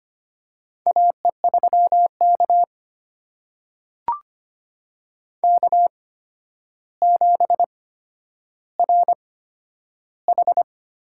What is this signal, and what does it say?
Morse code, heard as a keyed tone, about 25 wpm, spaced as for 11 wpm, 700 Hz. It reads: AE3K K7RH
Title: USA Call Signs - 4 Characters - 8x Character Spacing - 25wpm